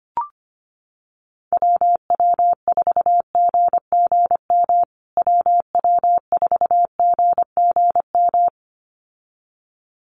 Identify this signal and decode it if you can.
Morse code, heard as a keyed tone, about 25 wpm, 700 Hz. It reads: WW4GGM WW4GGM
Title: USA Call Signs Repeated 2x 25wpm